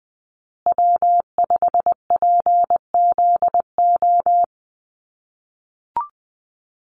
Morse code, heard as a keyed tone, about 20 wpm, 700 Hz. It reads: W5PZO
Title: USA Call Signs Repeated 2x 20wpm